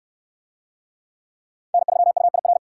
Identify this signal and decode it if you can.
Morse code, heard as a keyed tone, about 69 wpm, 700 Hz. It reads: N3UIR